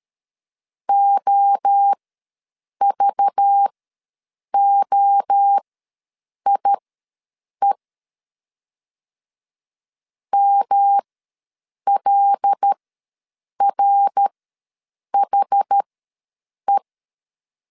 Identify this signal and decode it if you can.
Morse code, heard as a keyed tone, about 13 wpm, spaced as for 7 wpm, 800 Hz. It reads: OVOIE MLRHE
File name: LEZIONI / 5 LBVC / 5 LBV ws 12wpm 3015